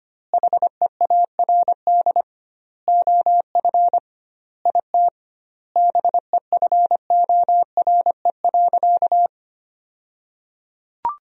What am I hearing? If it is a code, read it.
Morse code, heard as a keyed tone, about 25 wpm, 700 Hz. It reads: HEARD OF IT BEFORE.
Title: 2x Word-Spacing - Sentences from Top 400 Words - 25wpm